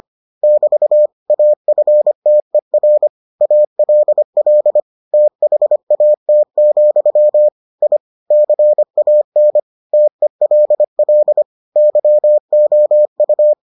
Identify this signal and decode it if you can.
Morse code, heard as a keyed tone, about 25 wpm, 600 Hz. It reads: <BT> AFTER ALL THAT, I CAN TELL YOU